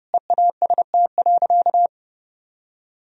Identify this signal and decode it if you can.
Morse code, heard as a keyed tone, about 29 wpm, 700 Hz. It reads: EAST.
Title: Sentences from Top 1000 Words - Review Part 3 - 30wpm